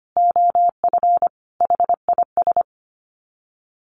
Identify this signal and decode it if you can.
Morse code, heard as a keyed tone, about 25 wpm, 700 Hz. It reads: OF HIS
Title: Sets of 4 Words 25wpm